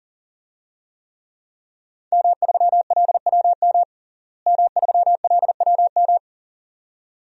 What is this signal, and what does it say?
Morse code, heard as a keyed tone, about 40 wpm, 700 Hz. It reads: M3LWM M3LWM
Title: DXCC-100 Call Signs - 5 Characters - Repeated 2x - 40wpm